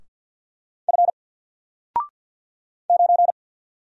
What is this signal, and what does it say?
Morse code, heard as a keyed tone, about 50 wpm, 700 Hz. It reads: F 9
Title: Single Letter-Number Alt 50wpm